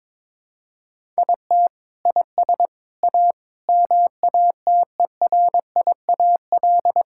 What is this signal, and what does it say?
Morse code, heard as a keyed tone, about 22 wpm, 700 Hz. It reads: IT IS A MATERIAL
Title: Sentences from Top 1000 Words - Review Part 5 - 22wpm